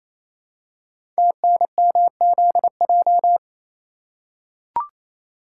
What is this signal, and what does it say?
Morse code, heard as a keyed tone, about 28 wpm, 700 Hz. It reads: TNMZJ
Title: Code Groups - 5 Letters - 28wpm